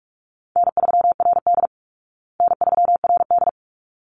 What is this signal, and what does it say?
Morse code, heard as a keyed tone, about 44 wpm, 700 Hz. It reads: N3RD N3RD
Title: Call Signs Repeated 3x - 45wpm